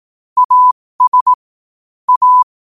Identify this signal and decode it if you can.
Morse code, heard as a keyed tone, about 18 wpm, spaced as for 15 wpm, 1000 Hz. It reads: AS A
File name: qotdfiles / 15wpm